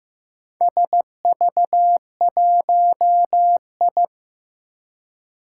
Morse code, heard as a keyed tone, about 15 wpm, 700 Hz. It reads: SV1I
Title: Four Challenging Letter-Number Combinations 15wpm